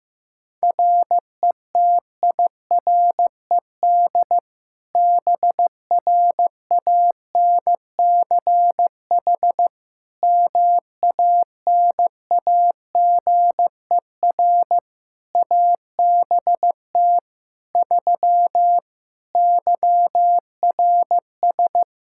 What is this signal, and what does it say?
Morse code, heard as a keyed tone, about 16 wpm, 700 Hz. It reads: RETIRED BRANCH MANAGER ABT 3 YRS
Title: Practice QSO - Exchange 3 - More Info - 15wpm